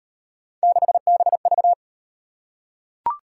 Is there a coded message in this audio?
Yes: Morse code from 0.6 to 1.7 s, heard as a keyed tone, about 35 wpm, 700 Hz. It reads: 6BV